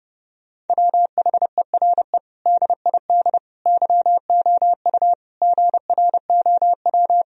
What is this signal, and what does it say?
Morse code, heard as a keyed tone, about 30 wpm, 700 Hz. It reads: WHERE DID YOU GROW
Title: Sentences from Top 1000 Words - Review Part 1 - 30wpm